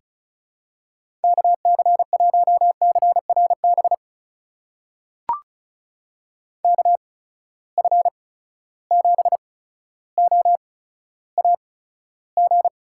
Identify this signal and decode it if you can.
Morse code, heard as a keyed tone, about 36 wpm, spaced as for 15 wpm, 700 Hz. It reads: KC1CRB KF7OAG
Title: USA Call Signs - 6 Characters - 8x Character Spacing - 35wpm